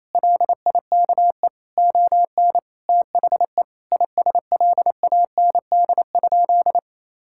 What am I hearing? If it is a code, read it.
Morse code, heard as a keyed tone, about 28 wpm, 700 Hz. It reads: LIKE ON THE ISLAND?
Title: Sentences from Top 1600 Words 28wpm